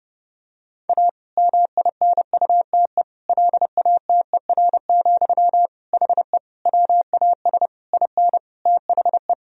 Morse code, heard as a keyed tone, about 31 wpm, 700 Hz. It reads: A MINUTE LATER, HE WAS IN THE
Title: Sentences from Top 1000 Words - Review Part 4 - 30wpm